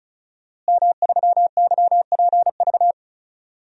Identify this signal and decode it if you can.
Morse code, heard as a keyed tone, about 34 wpm, 700 Hz. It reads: M3YPV
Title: DXCC-100 Call Signs - 5 Characters - Repeated 2x - 35wpm